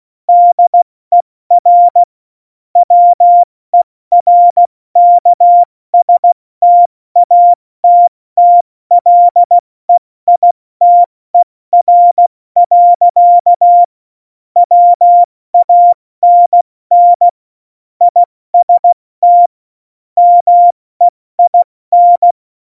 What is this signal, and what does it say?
Morse code, heard as a keyed tone, about 16 wpm, spaced as for 14 wpm, 700 Hz. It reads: DER WERKSTATTLEITER. WANN IST MEIN
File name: Morsekurs / Witze / Witz 02 Juni 2020, 70 F80